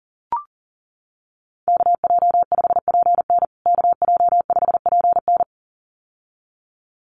Morse code, heard as a keyed tone, about 37 wpm, 700 Hz. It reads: KJ5PN KJ5PN
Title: USA Call Signs Repeated 2x 40wpm